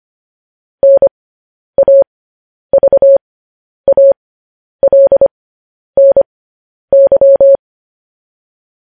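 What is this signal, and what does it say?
Morse code, heard as a keyed tone, about 25 wpm, spaced as for 10 wpm, 550 Hz. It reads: NAVALNY
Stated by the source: Morse code news headlines at 10.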